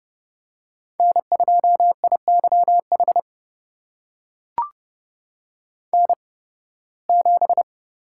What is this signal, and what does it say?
Morse code, heard as a keyed tone, about 30 wpm, spaced as for 20 wpm, 700 Hz. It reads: N2IYH N7
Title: USA Call Signs - 5 Characters - 8x Character Spacing - 30wpm